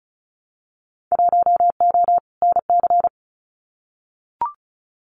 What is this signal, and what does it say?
Morse code, heard as a keyed tone, about 36 wpm, 700 Hz. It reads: 1O NC
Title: Winter Field Day Exchanges Encore 35wpm